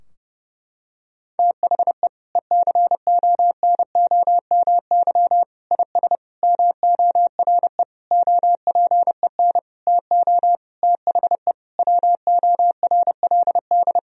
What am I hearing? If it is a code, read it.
Morse code, heard as a keyed tone, about 30 wpm, 700 Hz. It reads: THE ECONOMY IS MORE OPEN TO THE WORLD
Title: Sentences from Top 1000 Words - Review Part 3 - 30wpm